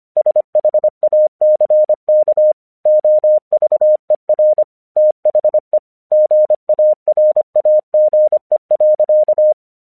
Morse code, heard as a keyed tone, about 24 wpm, 610 Hz. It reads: SHACK OVER THE GARAGE.
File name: 1 English 25wpm / AQC05